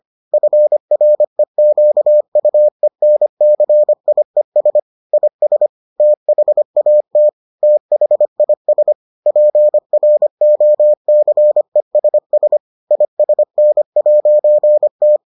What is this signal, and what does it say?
Morse code, heard as a keyed tone, about 25 wpm, 600 Hz. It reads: FREQUENCIES IS THAT THIS PROCESS ISN'T